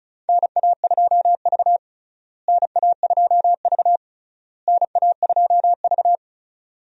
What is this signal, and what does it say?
Morse code, heard as a keyed tone, about 34 wpm, 700 Hz. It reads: NA2V NA2V NA2V
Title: USA Call Signs - 4 Characters - Repeated 3x - 35wpm